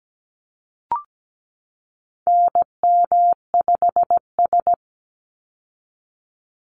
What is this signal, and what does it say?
Morse code, heard as a keyed tone, about 17 wpm, 700 Hz. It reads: NM5S
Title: Call Signs Repeated 3x - 17wpm